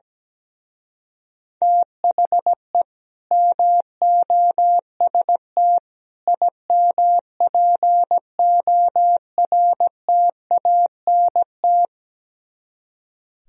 Morse code, heard as a keyed tone, about 17 wpm, 700 Hz. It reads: THE MOST IMPORTANT
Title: Speed-Racing - Sets of 3 Words 17wpm